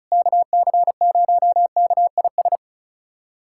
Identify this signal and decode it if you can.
Morse code, heard as a keyed tone, about 36 wpm, 700 Hz. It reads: KC0KIS
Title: USA Callsign DE Callsign - 6 Characters - 35wpm